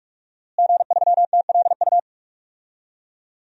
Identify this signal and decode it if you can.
Morse code, heard as a keyed tone, about 46 wpm, 700 Hz. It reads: G3TLU
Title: DXCC-100 Call Signs - 5 Characters - Repeated 3x - 45wpm